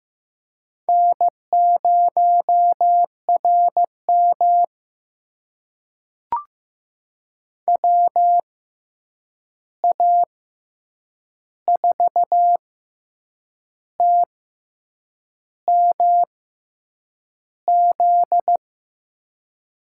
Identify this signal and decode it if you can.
Morse code, heard as a keyed tone, about 15 wpm, spaced as for 7 wpm, 700 Hz. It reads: N0RM WA4TMZ
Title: USA Call Signs - 6x Character Spacing - 15wpm